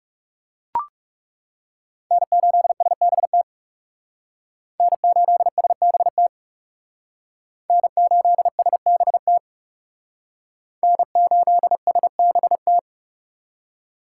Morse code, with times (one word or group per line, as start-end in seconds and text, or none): 2.1-3.4 N8SBT
4.8-6.3 N8SBT
7.7-9.4 N8SBT
10.8-12.8 N8SBT